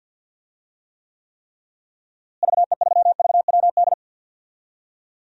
Morse code, heard as a keyed tone, about 51 wpm, 700 Hz. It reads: VE3VWD
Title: DXCC-100 Call Signs - 6 Characters - Repeated 3x - 50wpm